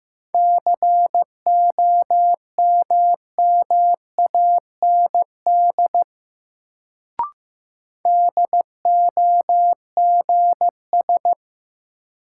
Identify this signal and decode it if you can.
Morse code, heard as a keyed tone, about 15 wpm, 700 Hz. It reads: COMMAND DOGS
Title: Sentences from Top 1600 Words - New Word Drill Alt - 15wpm